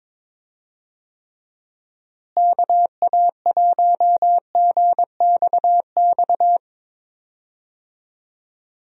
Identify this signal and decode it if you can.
Morse code, heard as a keyed tone, about 22 wpm, 700 Hz. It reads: KA1GXX
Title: USA Call Signs - 6 Characters - Repeated 2x - 22wpm